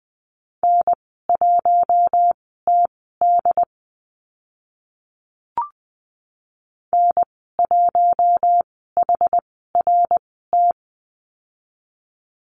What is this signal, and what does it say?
Morse code, heard as a keyed tone, about 19 wpm, spaced as for 14 wpm, 700 Hz. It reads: N1TD N1HRT
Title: 082 Lesson 14g - Number 2 - Call Signs 20wpm